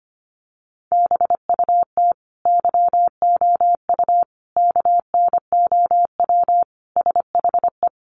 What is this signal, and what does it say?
Morse code, heard as a keyed tone, about 25 wpm, 700 Hz. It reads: BUT YOU KNOW SHE
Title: Sentences from Top 1400 Words 25wpm